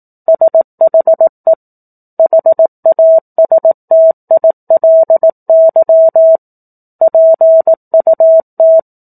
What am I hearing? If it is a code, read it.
Morse code, heard as a keyed tone, about 18 wpm, 650 Hz. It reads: SHE HASTILY PUT
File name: 04-Chapter-04 18 WPM